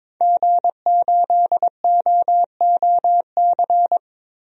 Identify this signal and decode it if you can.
Morse code, heard as a keyed tone, about 22 wpm, 700 Hz. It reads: G8OOC
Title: DXCC-100 Call Signs - 5 Characters - Repeated 3x - 22wpm